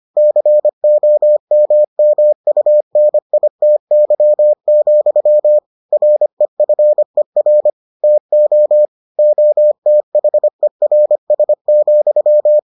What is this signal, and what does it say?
Morse code, heard as a keyed tone, about 25 wpm, 600 Hz. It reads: COMMUNITY, REFER TO OTHERS,